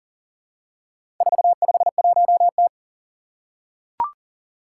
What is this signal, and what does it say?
Morse code, heard as a keyed tone, about 40 wpm, 700 Hz. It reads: 451T
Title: Four Challenging Letter-Number Combinations 40wpm